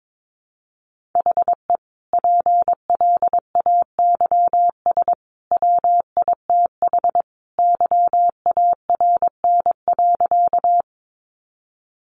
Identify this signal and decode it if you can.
Morse code, heard as a keyed tone, about 22 wpm, 700 Hz. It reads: HE PLAYS WITH YARN.
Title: Easy Phrases 22wpm